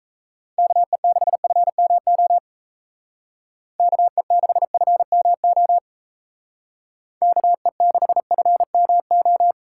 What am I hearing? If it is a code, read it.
Morse code, heard as a keyed tone, about 35 wpm, 700 Hz. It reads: KE6FMO KE6FMO KE6FMO
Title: Speed Racing - USA Call Signs 28wpm